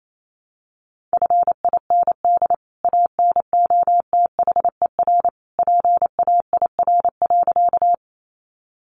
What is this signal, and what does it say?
Morse code, heard as a keyed tone, about 28 wpm, 700 Hz. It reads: FIND ANOTHER PAIR.
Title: Sentences from Top 1200 Words 28wpm